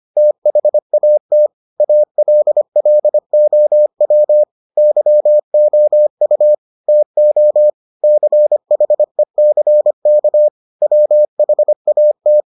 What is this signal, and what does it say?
Morse code, heard as a keyed tone, about 25 wpm, 600 Hz. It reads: THAT ALLOW YOU TO CHECK WHAT